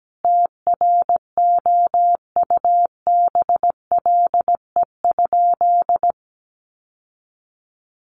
Morse code, heard as a keyed tone, about 17 wpm, 700 Hz. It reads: TROUBLE?
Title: Sentences from Top 1200 Words 17wpm